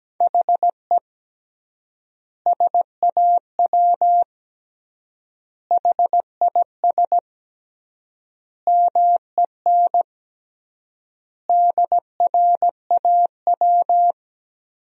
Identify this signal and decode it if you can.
Morse code, heard as a keyed tone, about 17 wpm, 700 Hz. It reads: HE SAW HIS MEN DRAW
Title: 3x Word-Spacing - Sentences from Top 300 Words - 17wpm